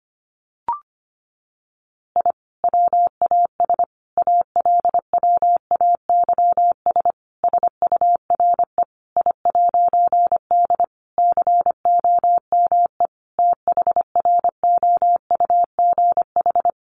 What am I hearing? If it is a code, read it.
Morse code, heard as a keyed tone, about 25 wpm, 700 Hz. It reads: I WAS ALWAYS SURE I'D COME THROUGH
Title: A Sound Of Thunder 25wpm